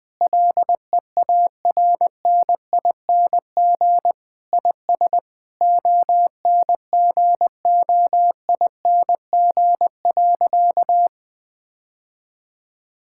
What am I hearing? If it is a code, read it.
Morse code, heard as a keyed tone, about 20 wpm, 700 Hz. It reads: LEARNING IS ONGOING.
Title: Sentences with 3-Word Phrases - 20wpm